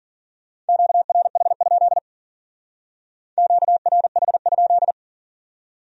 Morse code, heard as a keyed tone, about 52 wpm, 700 Hz. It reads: QRH? QRH?